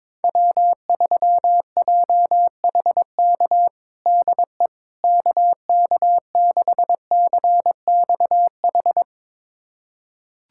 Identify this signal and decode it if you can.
Morse code, heard as a keyed tone, about 22 wpm, 700 Hz. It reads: W3JHK DE KK6CXH
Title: USA Callsign DE Callsign 22wpm